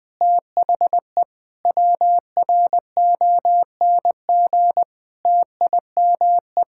Morse code, heard as a keyed tone, about 21 wpm, 700 Hz. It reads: THE WRONG TIME
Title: Sets of 8 Words - 20wpm